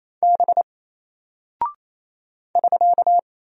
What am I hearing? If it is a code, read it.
Morse code, heard as a keyed tone, about 27 wpm, 700 Hz. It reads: B <SK>
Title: Single Character (Letter:Number:Punctuation:Prosign) 28wpm